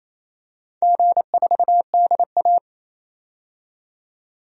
Morse code, heard as a keyed tone, about 28 wpm, 700 Hz. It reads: G4DA
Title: DXCC-100 Call Signs - 4 Characters - 28wpm